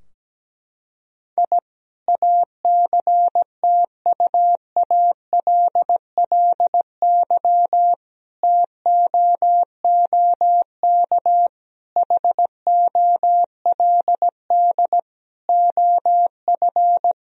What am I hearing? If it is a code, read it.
Morse code, heard as a keyed tone, about 17 wpm, 700 Hz. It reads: I ACTUALLY TOOK HOLD OF
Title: Sentences from Top 1000 Words - Review Part 4 - 17wpm